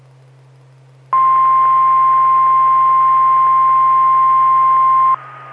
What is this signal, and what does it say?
Emergency Alert System (SAME) tones (the 1050 Hz attention tone). It is heard from 1.1 to 5.2 s.
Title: Saline, AK SKYWARN Repeater System - W5RHS